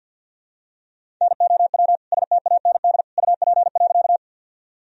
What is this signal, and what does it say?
Morse code, heard as a keyed tone, about 49 wpm, 700 Hz. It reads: NOW STAND UP.
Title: Sentences from Top 1000 Words - Review Part 1 - 50wpm